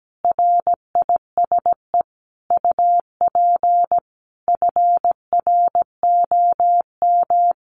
Morse code, heard as a keyed tone, about 17 wpm, 700 Hz. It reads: RISE UP FROM
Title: Sentences from Top 900 Words 17wpm